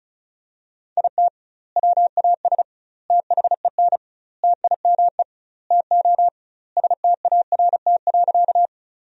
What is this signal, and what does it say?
Morse code, heard as a keyed tone, about 34 wpm, 700 Hz. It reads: IT WAS THEN TIME TO START.
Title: 2x Word-Spacing - Sentences from Top 300 Words - 35wpm